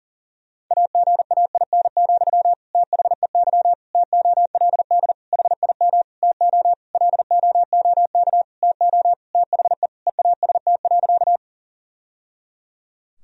Morse code, heard as a keyed tone, about 40 wpm, 700 Hz. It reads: AGAIN, THEY TOLD HIM TO LOOK TO THE EAST.
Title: Sentences from Top 1000 Words - Review Part 3 - 40wpm